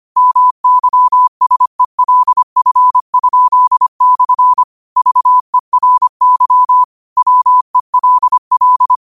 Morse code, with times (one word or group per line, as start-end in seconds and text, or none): 0.2-4.6 MYSELF?/
5.0-6.8 VERY
7.2-9.0 WELL